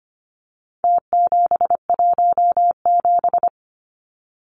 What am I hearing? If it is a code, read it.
Morse code, heard as a keyed tone, about 25 wpm, 700 Hz. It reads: T717
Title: Numbers - Contest Style - 4-Digit 25wpm